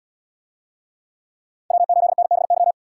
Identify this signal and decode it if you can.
Morse code, heard as a keyed tone, about 75 wpm, 700 Hz. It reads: W8NBY